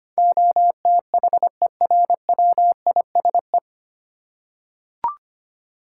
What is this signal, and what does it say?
Morse code, heard as a keyed tone, about 25 wpm, 700 Hz. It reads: OTHERWISE
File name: Nine Letter Words - 25wpm